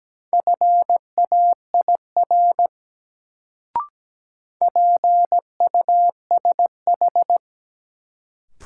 Morse code, heard as a keyed tone, about 17 wpm, 700 Hz. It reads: FAIR PUSH
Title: Sentences from Top 1300 Words - New Word Drill Alt - 17wpm